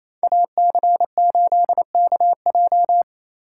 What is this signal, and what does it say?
Morse code, heard as a keyed tone, about 28 wpm, 700 Hz. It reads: AC8KJ
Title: USA Call Signs - 2x Character Spacing - 28wpm